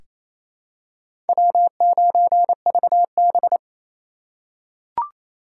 Morse code, heard as a keyed tone, about 28 wpm, 700 Hz. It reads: W9VB
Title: USA Call Signs - 4x Character Spacing - 28wpm